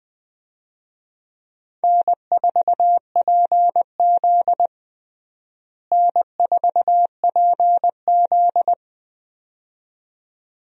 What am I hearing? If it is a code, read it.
Morse code, heard as a keyed tone, about 20 wpm, 700 Hz. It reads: N4PZ N4PZ
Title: USA Call Signs - 4 Characters - Repeated 2x - 20wpm